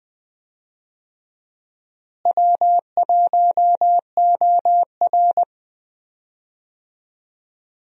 Morse code, heard as a keyed tone, about 20 wpm, 700 Hz. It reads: W1OR
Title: USA Call Signs - 4 Characters - Repeated 3x - 20wpm